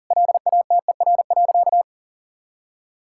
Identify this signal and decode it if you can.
Morse code, heard as a keyed tone, about 40 wpm, 700 Hz. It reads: LATER.
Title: Sentences from Top 1000 Words - Review Part 5 - 40wpm